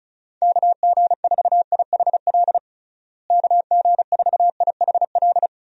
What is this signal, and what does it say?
Morse code, heard as a keyed tone, about 36 wpm, 700 Hz. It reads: KG4IHL KG4IHL
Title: USA Call Signs - 6 Characters - Repeated 2x - 35wpm